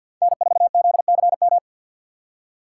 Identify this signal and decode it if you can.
Morse code, heard as a keyed tone, about 52 wpm, 700 Hz. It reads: N4ZCM
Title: USA Call Signs - 2x Character Spacing - 50wpm